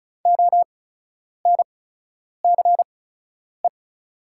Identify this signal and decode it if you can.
Morse code, heard as a keyed tone, about 36 wpm, spaced as for 10 wpm, 690 Hz. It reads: ONCE
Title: 8x Character Spacing - Top 500 Words - 35wpm